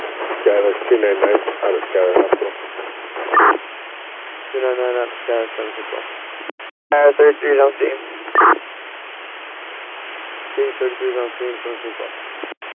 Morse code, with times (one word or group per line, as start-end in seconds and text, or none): none